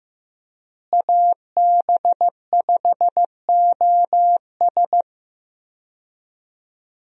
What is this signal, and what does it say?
Morse code, heard as a keyed tone, about 16 wpm, 700 Hz. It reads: AB5OS
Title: USA Call Signs - 5 Characters - Repeated 2x - 15wpm